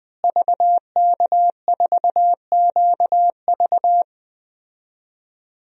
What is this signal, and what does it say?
Morse code, heard as a keyed tone, about 21 wpm, 700 Hz. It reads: VK4QV
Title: DXCC-100 Call Signs - 5 Characters - Repeated 3x - 20wpm